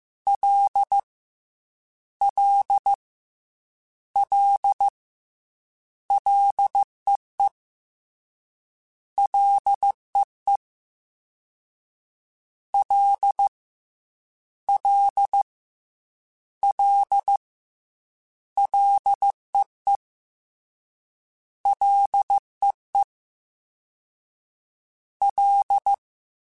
Morse code, heard as a keyed tone, about 15 wpm, 800 Hz. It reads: L L L LEE LEE L L L LEE LEE L
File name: Ls / Single L m3s / Single L Prefixe 3x2 Twice @13